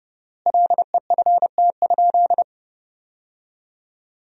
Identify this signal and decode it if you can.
Morse code, heard as a keyed tone, about 30 wpm, 700 Hz. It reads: LEFT?
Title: 4x Word-Spacing - Sentences from Top 300 Words - 30wpm